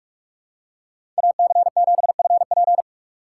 Morse code, heard as a keyed tone, about 44 wpm, 700 Hz. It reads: AC7FP
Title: USA Call Signs - 2x Character Spacing - 45wpm